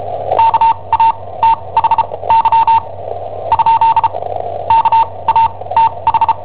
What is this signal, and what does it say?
Morse code, heard as a keyed tone, about 34 wpm, spaced as for 22 wpm, 910 Hz. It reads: KATHY ? KATH
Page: Cassetta esercizi preparata da I7OHP per IW5ELP